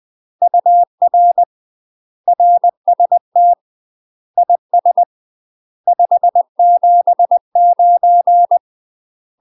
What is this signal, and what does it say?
Morse code, heard as a keyed tone, about 20 wpm, 700 Hz. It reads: UR RST IS 579
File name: UK MASTER QSOs TRAINING / 20WPM / UK MASTER QSOs 0023